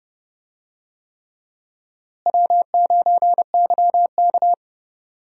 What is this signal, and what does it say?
Morse code, heard as a keyed tone, about 30 wpm, 700 Hz. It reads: W9YK
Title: USA Call Signs - 4 Characters - Repeated 3x - 30wpm